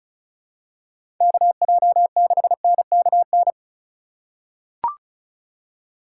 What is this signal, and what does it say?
Morse code, heard as a keyed tone, about 34 wpm, 700 Hz. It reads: KJ6NKN
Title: USA Call Signs - 6 Characters - 2x Character Spacing - 35wpm